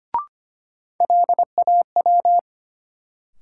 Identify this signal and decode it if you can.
Morse code, heard as a keyed tone, about 25 wpm, 700 Hz. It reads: LAW
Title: Sentences from Top 600 Words - New Word Drill - 25wpm